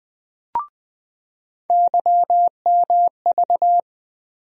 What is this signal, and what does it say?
Morse code, heard as a keyed tone, about 21 wpm, 700 Hz. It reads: YMV
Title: USA Call Sign Postfixes - 3 Letters 20wpm